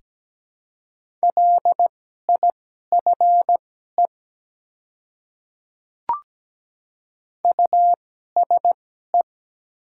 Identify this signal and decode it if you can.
Morse code, heard as a keyed tone, about 17 wpm, spaced as for 12 wpm, 700 Hz. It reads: LIFE USE
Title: 2x Character Spacing - Top 300 Words - 17wpm